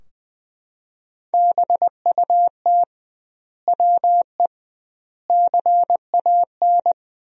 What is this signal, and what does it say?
Morse code, heard as a keyed tone, about 21 wpm, 700 Hz. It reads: BUT WE CAN
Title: 2x Word-Spacing - Sentences from Top 100 Words - 20wpm